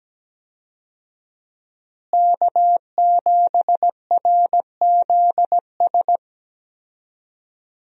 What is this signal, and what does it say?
Morse code, heard as a keyed tone, about 17 wpm, 700 Hz. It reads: K7RZS
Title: USA Call Signs - 5 Characters - Repeated 3x - 17wpm